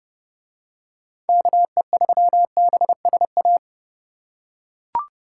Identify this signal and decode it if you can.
Morse code, heard as a keyed tone, about 29 wpm, 700 Hz. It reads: KE3BSA